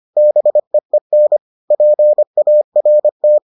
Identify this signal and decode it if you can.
Morse code, heard as a keyed tone, about 25 wpm, 600 Hz. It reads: BEEN PART